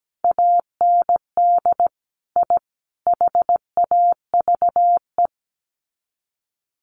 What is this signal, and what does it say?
Morse code, heard as a keyed tone, about 17 wpm, 700 Hz. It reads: AND I HAVE
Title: Sets of 3 Words 17wpm